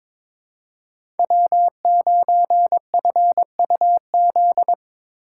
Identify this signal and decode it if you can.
Morse code, heard as a keyed tone, about 22 wpm, 700 Hz. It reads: W9FUZ
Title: USA Call Signs - 5 Characters - 4x Character Spacing - 22wpm